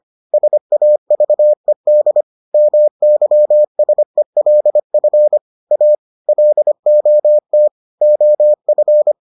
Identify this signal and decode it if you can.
Morse code, heard as a keyed tone, about 25 wpm, 600 Hz. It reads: SAVED MYSELF A LOT OF